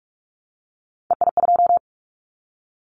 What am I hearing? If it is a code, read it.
Morse code, heard as a keyed tone, about 46 wpm, 690 Hz. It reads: EI2
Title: Callsign Prefixes for Top 50 DXCC - 45wpm